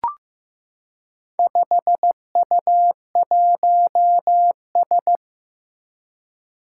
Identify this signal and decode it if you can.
Morse code, heard as a keyed tone, about 15 wpm, 700 Hz. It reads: 5U1S
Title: Four Challenging Letter-Number Combinations 15wpm